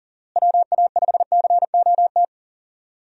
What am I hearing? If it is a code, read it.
Morse code, heard as a keyed tone, about 37 wpm, 700 Hz. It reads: WA5COT
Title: USA Call Signs - 6 Characters - Repeated 3x - 40wpm